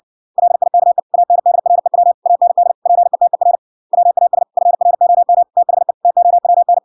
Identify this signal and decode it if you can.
Morse code, heard as a keyed tone, about 60 wpm, 700 Hz. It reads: LIKE ANDREW AND PETER WAS FROM THE TOWN